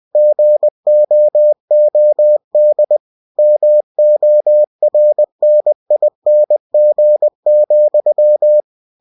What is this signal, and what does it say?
Morse code, heard as a keyed tone, about 20 wpm, 600 Hz. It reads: GOOD MORNING,
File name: cw-books / winnie-the-pooh / winnie-20wpm-page-0050